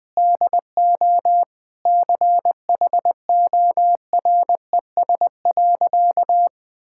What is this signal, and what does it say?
Morse code, heard as a keyed tone, about 20 wpm, 700 Hz. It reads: DO CHORES.
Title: Sentences with 5-Word Phrases - 20wpm